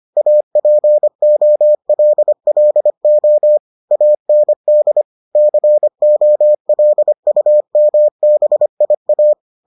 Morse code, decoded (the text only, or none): APOLLO AND COLUMBIA